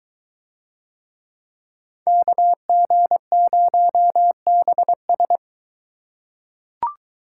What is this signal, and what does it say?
Morse code, heard as a keyed tone, about 20 wpm, 700 Hz. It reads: KG0BS